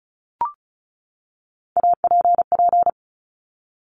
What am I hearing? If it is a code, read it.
Morse code, heard as a keyed tone, about 36 wpm, 700 Hz. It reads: APP
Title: Top Three Letter Combinations in Words 35wpm